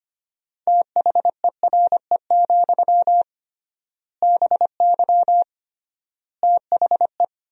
Morse code, heard as a keyed tone, about 25 wpm, 700 Hz. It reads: THERE, BY THE
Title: 3x Word-Spacing - Sentences from Top 500 Words - 25wpm